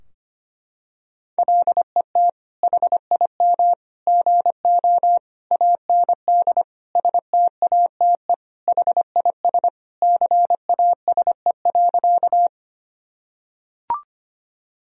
Morse code, heard as a keyed tone, about 25 wpm, 700 Hz. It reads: LET HIM GO AND STATE HIS CASE.
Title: Sentences from Top 1000 Words - Review Part 2 - 25wpm